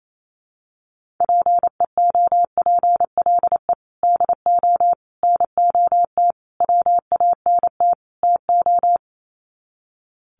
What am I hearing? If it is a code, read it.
Morse code, heard as a keyed tone, about 28 wpm, 700 Hz. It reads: PEOPLE DO NOT WANT TO
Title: Sets of 5 Words Encore2 28wpm